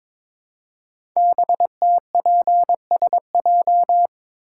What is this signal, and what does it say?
Morse code, heard as a keyed tone, about 22 wpm, 700 Hz. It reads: BTPSJ
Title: Code Groups - 5 Letters - 22wpm